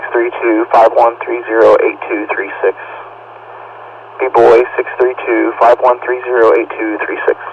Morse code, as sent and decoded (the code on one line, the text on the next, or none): none